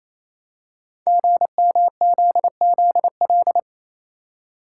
Morse code, heard as a keyed tone, about 27 wpm, 700 Hz. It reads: GMZZL
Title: Code Groups - 5 Letters - 28wpm